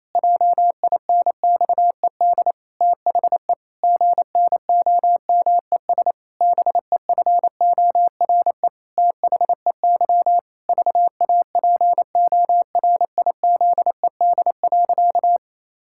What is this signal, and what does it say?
Morse code, heard as a keyed tone, about 28 wpm, 700 Hz. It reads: JINXED THE GNOMES BEFORE THEY VAPORIZED.
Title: Pangram Challenge v2 - 28wpm